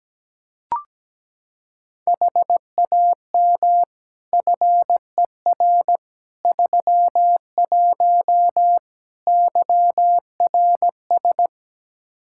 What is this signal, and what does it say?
Morse code, heard as a keyed tone, about 17 wpm, 700 Hz. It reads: HAM FER 31 YRS
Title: K1USN SST QSO Practice 17wpm